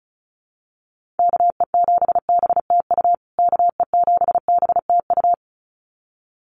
Morse code, heard as a keyed tone, about 34 wpm, 700 Hz. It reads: KE7BTU KE7BTU
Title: USA Call Signs Repeated 2x 35wpm